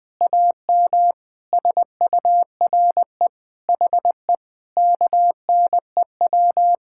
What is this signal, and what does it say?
Morse code, heard as a keyed tone, about 20 wpm, 700 Hz. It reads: AM SURE HE KNEW
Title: Sentences from Top 1000 Words - Review Part 6 - 20wpm